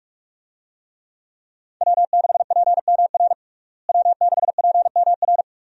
Morse code, heard as a keyed tone, about 46 wpm, 700 Hz. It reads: W6PMR W6PMR
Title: USA Call Signs - 5 Characters - Repeated 3x - 45wpm